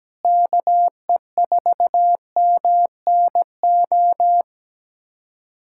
Morse code, heard as a keyed tone, about 17 wpm, 700 Hz. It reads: KE4MNO
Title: USA Callsign DE Callsign 17wpm